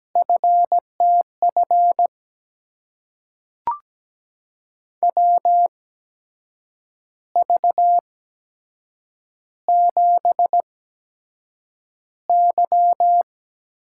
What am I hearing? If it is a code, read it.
Morse code, heard as a keyed tone, about 17 wpm, spaced as for 7 wpm, 700 Hz. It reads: FTF WV7Y
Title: USA Call Signs - 8x Character Spacing - 17wpm